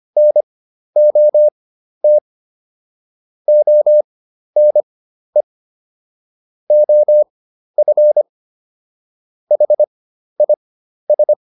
Morse code, heard as a keyed tone, about 25 wpm, spaced as for 12 wpm, 600 Hz. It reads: NOT ONE OF HIS